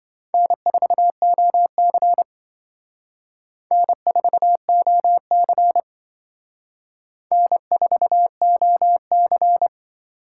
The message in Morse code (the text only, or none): N4OC N4OC N4OC